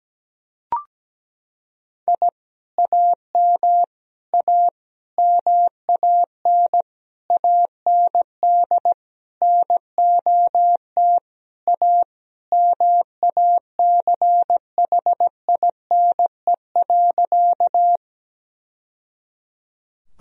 Morse code, heard as a keyed tone, about 17 wpm, 700 Hz. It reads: I AM A MAN AND NOT A MACHINE.
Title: Sentences from Top 1000 Words - Review Part 2 - 17wpm